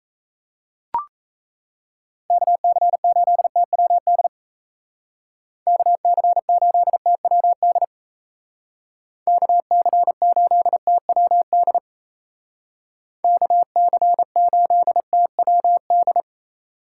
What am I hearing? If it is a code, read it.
Morse code, heard as a keyed tone, about 33 wpm, 700 Hz. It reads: KC8TWD KC8TWD KC8TWD KC8TWD
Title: Speed Racing - USA 2x3 Call Signs 28wpm